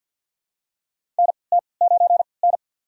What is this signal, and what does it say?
Morse code, heard as a keyed tone, about 52 wpm, spaced as for 29 wpm, 700 Hz. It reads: NT9N
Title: USA Call Signs - 4 Characters - 3x Character Spacing - 50wpm